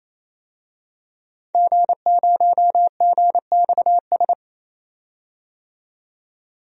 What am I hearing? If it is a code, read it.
Morse code, heard as a keyed tone, about 28 wpm, 700 Hz. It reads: G0GXS
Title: DXCC-100 Call Signs - 5 Characters - 28wpm